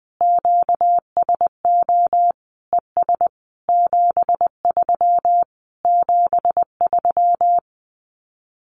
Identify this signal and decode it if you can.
Morse code, heard as a keyed tone, about 21 wpm, 700 Hz. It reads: QSO ES 73 73
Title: Common QSO Phrases Encore 20wpm